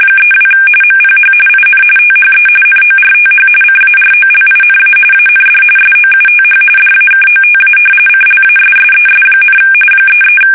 RTTY, about 45.45 baud. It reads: 32 MPH IN THE NORTH CAROLINA AND SOUTH DAKOTA CLASSES, 33
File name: fsk850 / battleships